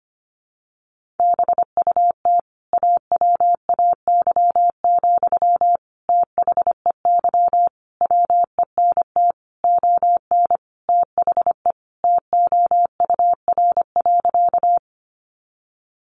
Morse code, heard as a keyed tone, about 25 wpm, 700 Hz. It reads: BUT AWAY, THEY WENT ON THE TOUR.
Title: Sentences from Top 1400 Words 25wpm